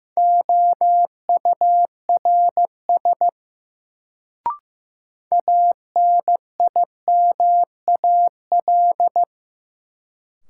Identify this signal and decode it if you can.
Morse code, heard as a keyed tone, about 15 wpm, 700 Hz. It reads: OURS ANIMAL
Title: Sentences from Top 1300 Words - New Word Drill Alt - 15wpm